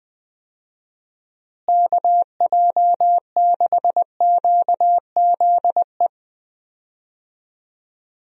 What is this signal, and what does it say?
Morse code, heard as a keyed tone, about 20 wpm, 700 Hz. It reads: KJ6QZE
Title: USA Call Signs - 6 Characters - Repeated 2x - 20wpm